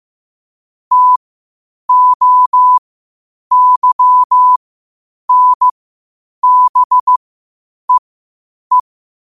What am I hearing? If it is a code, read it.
Morse code, heard as a keyed tone, about 15 wpm, spaced as for 8 wpm, 1000 Hz. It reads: TOYNBEE
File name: qotdfiles / 8wpm